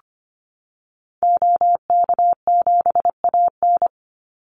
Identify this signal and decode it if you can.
Morse code, heard as a keyed tone, about 25 wpm, 700 Hz. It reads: OK7AN
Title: Call Signs Repeated 3x - 25wpm